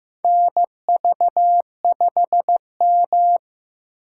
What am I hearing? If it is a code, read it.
Morse code, heard as a keyed tone, about 15 wpm, 700 Hz. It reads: NV5M
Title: USA Call Signs - 4 Characters - 4x Character Spacing - 15wpm